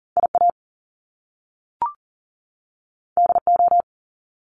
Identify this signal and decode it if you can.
Morse code, heard as a keyed tone, about 38 wpm, 700 Hz. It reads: IA DO
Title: Top Two Letter Combinations in Words 40wpm